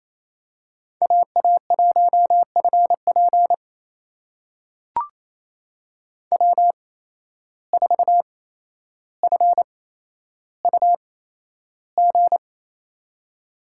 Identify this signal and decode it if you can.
Morse code, heard as a keyed tone, about 28 wpm, spaced as for 12 wpm, 700 Hz. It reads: AA1FP W4FUG
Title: USA Call Signs - 5 Characters - 8x Character Spacing - 28wpm